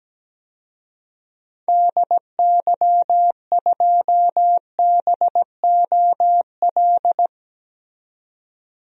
Morse code, heard as a keyed tone, about 17 wpm, 700 Hz. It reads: DY2BOL
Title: DXCC-100 Call Signs - 6 Characters - 17wpm